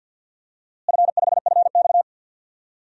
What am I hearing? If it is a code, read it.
Morse code, heard as a keyed tone, about 47 wpm, 700 Hz. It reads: F5FX
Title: DXCC-100 Call Signs - 4 Characters - Repeated 3x - 50wpm